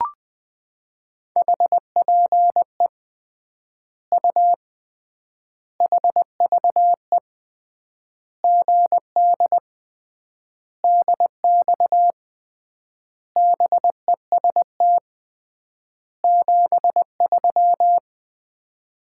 Morse code, heard as a keyed tone, about 20 wpm, 700 Hz. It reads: HPE U HVE GD DX BEST 73
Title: Common QSO Phrases - 3x Word-Spacing - 20wpm